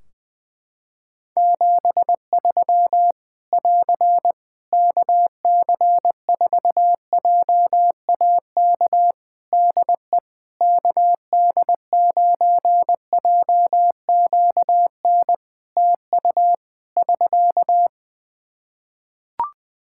Morse code, heard as a keyed tone, about 21 wpm, 700 Hz. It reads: 73 <AR> KC4JAK DE KD9JQN TU <SK>
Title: Practice QSO - Exchange 4 - Ending - 20wpm